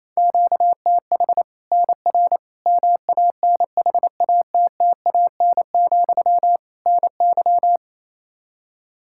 Morse code, heard as a keyed tone, about 28 wpm, 700 Hz. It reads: QTH NR MANHATTAN, NY
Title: Practice QSO - Exchange 1 - Beginning - 28wpm